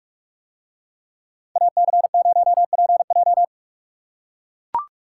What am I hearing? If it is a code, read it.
Morse code, heard as a keyed tone, about 40 wpm, 700 Hz. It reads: AC0PJ